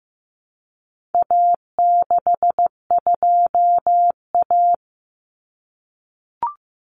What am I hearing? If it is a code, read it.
Morse code, heard as a keyed tone, about 15 wpm, 700 Hz. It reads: A62A
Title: Common Cut Numbers - 4-Digit 15wpm